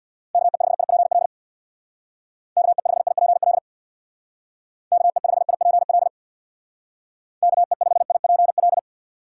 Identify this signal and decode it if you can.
Morse code, heard as a keyed tone, about 62 wpm, 700 Hz. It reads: KE5IPL KE5IPL KE5IPL KE5IPL
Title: Speed Racing - USA 2x3 Call Signs 50wpm